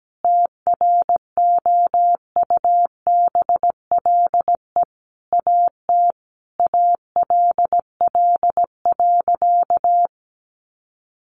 Morse code, heard as a keyed tone, about 17 wpm, 700 Hz. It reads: TROUBLE AT ALL.
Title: Sentences from Top 1200 Words 17wpm